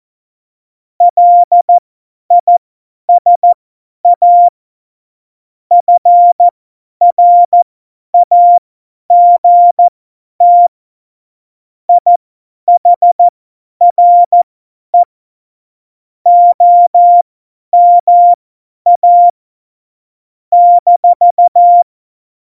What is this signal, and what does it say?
Morse code, heard as a keyed tone, about 14 wpm, spaced as for 10 wpm, 700 Hz. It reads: LISA FRAGT IHRE OMA -
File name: Morsekurs / Witze / Witz 07. November 2017, 50 F70